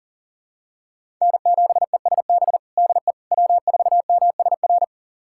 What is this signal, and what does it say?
Morse code, heard as a keyed tone, about 40 wpm, 700 Hz. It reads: N7ESB DE W4MSR
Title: USA Callsign DE Callsign - 5 Characters - 40wpm